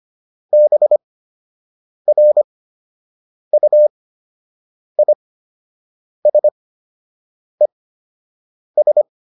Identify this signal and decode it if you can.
Morse code, heard as a keyed tone, about 25 wpm, spaced as for 7 wpm, 600 Hz. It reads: BRUISES